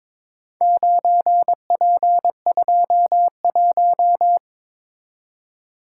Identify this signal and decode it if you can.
Morse code, heard as a keyed tone, about 21 wpm, 700 Hz. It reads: 9P21
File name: practice / 22wpm / Four Letter-Number Combinations - 22wpm